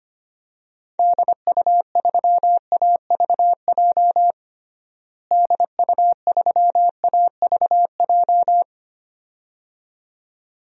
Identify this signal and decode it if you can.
Morse code, heard as a keyed tone, about 25 wpm, 700 Hz. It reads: DU3AVJ DU3AVJ
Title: DXCC-100 Call Signs - 6 Characters - Repeated 2x - 25wpm